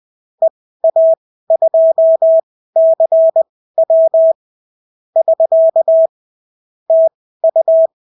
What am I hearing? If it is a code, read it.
Morse code, heard as a keyed tone, about 20 wpm, spaced as for 14 wpm, 650 Hz. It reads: EA2CW <SK> TU
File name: CW QSO libroCW4 20-10ppm